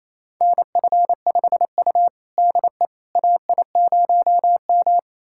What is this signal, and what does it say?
Morse code, heard as a keyed tone, about 27 wpm, 700 Hz. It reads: NF5U DE AI0M
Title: USA Callsign DE Callsign - 4 Characters - 28wpm